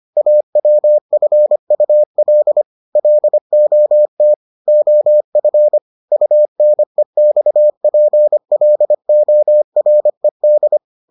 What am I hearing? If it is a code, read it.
Morse code, heard as a keyed tone, about 25 wpm, 600 Hz. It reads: AWFUL LOT OF UNEXPLORED